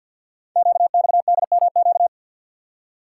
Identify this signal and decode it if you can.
Morse code, heard as a keyed tone, about 48 wpm, 690 Hz. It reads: QXDMQ
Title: Code Groups - 5 Letters - 50wpm